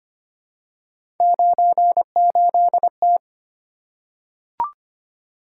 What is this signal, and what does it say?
Morse code, heard as a keyed tone, about 25 wpm, 700 Hz. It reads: 98T
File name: Three Letter-Number Combinations - 25wpm